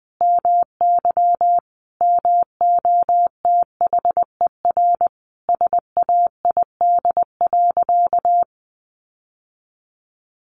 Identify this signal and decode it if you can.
Morse code, heard as a keyed tone, about 20 wpm, 700 Hz. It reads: MY MOTHER SAID.
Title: Sentences from Top 1400 Words 20wpm